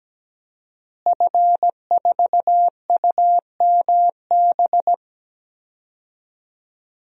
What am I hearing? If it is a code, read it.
Morse code, heard as a keyed tone, about 17 wpm, 700 Hz. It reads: F4UMB
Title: DXCC-100 Call Signs - 5 Characters - Repeated 3x - 17wpm